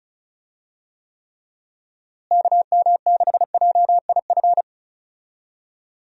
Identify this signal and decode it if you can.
Morse code, heard as a keyed tone, about 34 wpm, 700 Hz. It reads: KM6JIF
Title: USA Call Signs - 6 Characters - Repeated 2x - 35wpm